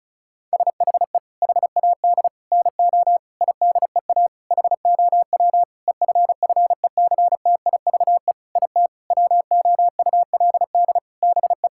Morse code, heard as a keyed tone, about 34 wpm, 700 Hz. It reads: SHE HAD NO IDEA HOW EFFECTIVE IT WOULD BE
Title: Sentences from Top 1000 Words - Review Part 6 - 35wpm